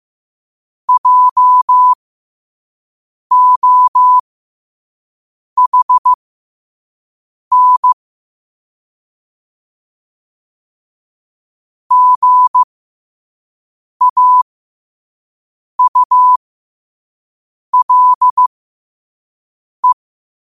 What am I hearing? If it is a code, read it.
Morse code, heard as a keyed tone, about 15 wpm, spaced as for 5 wpm, 1000 Hz. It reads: JOHN GAULE
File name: qotdfiles / 5wpm